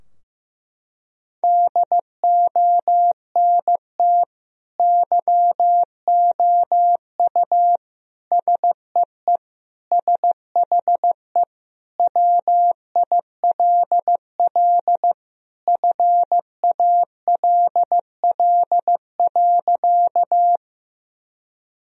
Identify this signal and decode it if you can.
Morse code, heard as a keyed tone, about 15 wpm, 700 Hz. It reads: DONT YOU SEE SHE WILL FALL.
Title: Sentences from Top 1000 Words - Review Part 2 - 15wpm